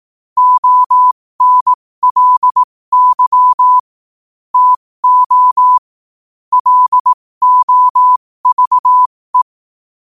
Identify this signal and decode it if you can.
Morse code, heard as a keyed tone, about 18 wpm, spaced as for 15 wpm, 1000 Hz. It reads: ONLY TO LOVE
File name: qotdfiles / 15wpm